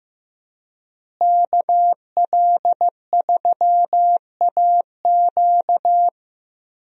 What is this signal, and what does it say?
Morse code, heard as a keyed tone, about 15 wpm, 700 Hz. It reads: KL3AQ
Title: USA Call Signs - 5 Characters - Repeated 3x - 15wpm